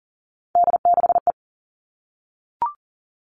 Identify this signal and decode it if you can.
Morse code, heard as a keyed tone, about 40 wpm, 700 Hz. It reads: D6E
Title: All Cut Numbers - 3-Digit 40wpm